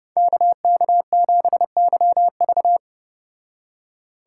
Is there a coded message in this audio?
Yes: Morse code from 0.2 to 2.8 s, heard as a keyed tone, about 29 wpm, 700 Hz. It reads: KK7YV